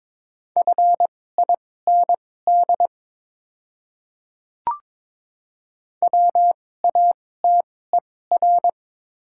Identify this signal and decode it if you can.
Morse code, heard as a keyed tone, about 22 wpm, spaced as for 16 wpm, 700 Hz. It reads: FIND WATER
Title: 2x Character Spacing - Top 100 Words - 22wpm